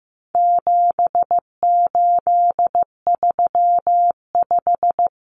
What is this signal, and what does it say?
Morse code, heard as a keyed tone, about 15 wpm, 700 Hz. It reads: 7835
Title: Numbers - 4-Digit 15wpm